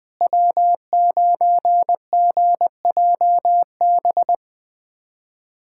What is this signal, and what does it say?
Morse code, heard as a keyed tone, about 20 wpm, 700 Hz. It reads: W9GJB
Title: USA Callsign DE Callsign - 5 Characters - 20wpm